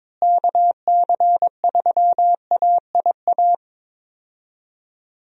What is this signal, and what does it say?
Morse code, heard as a keyed tone, about 22 wpm, 700 Hz. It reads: KC3AIA
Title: USA Call Signs - 6 Characters - Repeated 3x - 22wpm